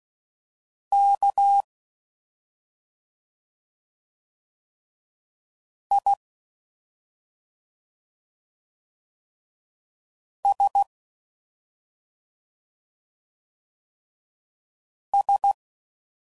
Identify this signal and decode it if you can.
Morse code, heard as a keyed tone, about 16 wpm, 800 Hz. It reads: K I S S